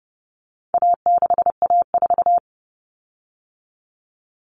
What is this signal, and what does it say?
Morse code, heard as a keyed tone, about 30 wpm, 700 Hz. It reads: A6A4
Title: Common Cut Numbers - 4-Digit 30wpm